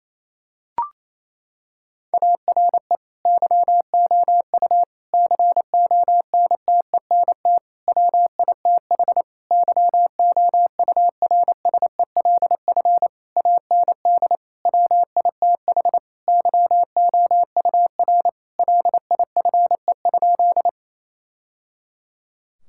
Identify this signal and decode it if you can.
Morse code, heard as a keyed tone, about 28 wpm, 700 Hz. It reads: ARE YOU CONTENT WITH YOURSELF AND WITH YOUR LIFE?
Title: Sentences from Top 1000 Words - Review Part 6 - 28wpm